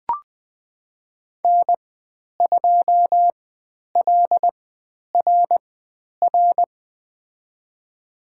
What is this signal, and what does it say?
Morse code, heard as a keyed tone, about 20 wpm, spaced as for 10 wpm, 700 Hz. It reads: N2LRR
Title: 102 Lesson 17g - Letter u - Call Signs 20wpm